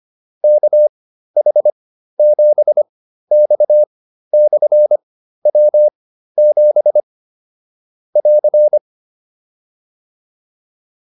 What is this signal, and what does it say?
Morse code, heard as a keyed tone, about 25 wpm, spaced as for 13 wpm, 600 Hz. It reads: KH7X/W7 <AR>
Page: intermediate-practice-files